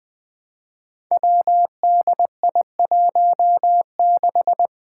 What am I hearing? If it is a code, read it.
Morse code, heard as a keyed tone, about 20 wpm, 700 Hz. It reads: WDI16
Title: Five Challenging Letter-Number Combinations 20wpm